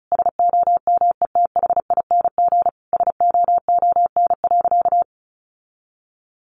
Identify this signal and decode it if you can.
Morse code, heard as a keyed tone, about 34 wpm, 700 Hz. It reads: SOMETHING SOON.
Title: Sentences from Top 1500 Words 35wpm